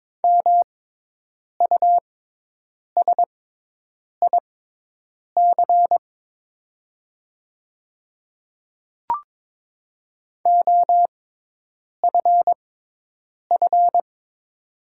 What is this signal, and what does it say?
Morse code, heard as a keyed tone, about 22 wpm, spaced as for 8 wpm, 690 Hz. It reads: MUSIC OFF
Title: 6x Character Spacing - Top 300 Words - 22wpm